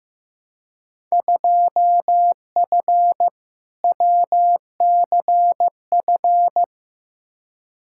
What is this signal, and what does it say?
Morse code, heard as a keyed tone, about 15 wpm, 700 Hz. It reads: 2F WCF
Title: Field Day Exchanges Encore - 15wpm